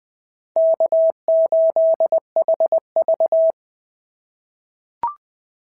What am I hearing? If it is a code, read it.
Morse code, heard as a keyed tone, about 20 wpm, 690 Hz. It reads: K8HV
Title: SOTA QSO Practice 20wpm